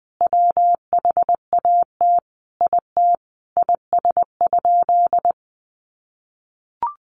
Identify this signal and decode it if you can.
Morse code, heard as a keyed tone, about 20 wpm, 700 Hz. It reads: WHAT IT IS?
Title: Sentences from Top 1500 Words 20wpm